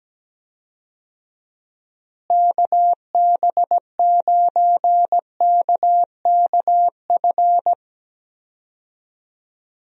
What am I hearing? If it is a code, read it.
Morse code, heard as a keyed tone, about 17 wpm, 700 Hz. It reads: KB9KKF
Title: USA Call Signs - 6 Characters - Repeated 2x - 17wpm